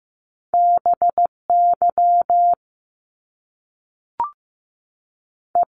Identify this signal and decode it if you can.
Morse code, heard as a keyed tone, about 15 wpm, 700 Hz. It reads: BY E
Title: Top Two Letter Combinations in Words 15wpm